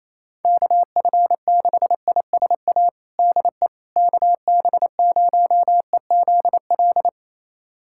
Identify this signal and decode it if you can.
Morse code, heard as a keyed tone, about 28 wpm, 700 Hz. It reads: KF6ISA DE KB0EZL
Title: USA Callsign DE Callsign - 6 Characters - 28wpm